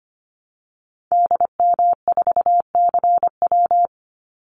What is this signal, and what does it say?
Morse code, heard as a keyed tone, about 25 wpm, 700 Hz. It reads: DM4CW
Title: Call Signs - 25wpm